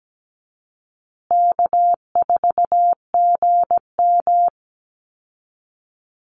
Morse code, heard as a keyed tone, about 17 wpm, 700 Hz. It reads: K4GM
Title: Easy Call Signs Repeated 2x - 17wpm